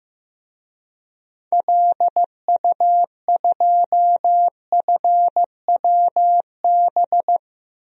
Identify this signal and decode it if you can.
Morse code, heard as a keyed tone, about 15 wpm, 700 Hz. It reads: LU2FWB
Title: DXCC-100 Call Signs - 6 Characters - Repeated 2x - 15wpm